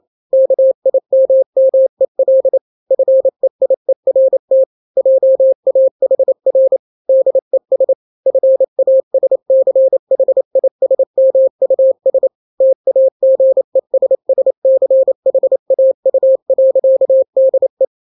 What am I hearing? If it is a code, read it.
Morse code, heard as a keyed tone, about 28 wpm, 530 Hz. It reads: KIMMEL FEIERT JAHR DES FASCHISMUS TAGESSCHAU.DE
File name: daily / 27wpm